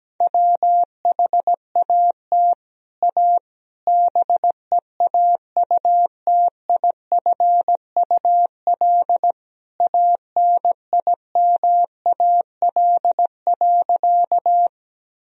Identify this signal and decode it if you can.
Morse code, heard as a keyed tone, about 17 wpm, 700 Hz. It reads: WHAT A BEAUTIFUL ANIMAL.
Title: Sentences from Top 1000 Words - Review Part 1 - 17wpm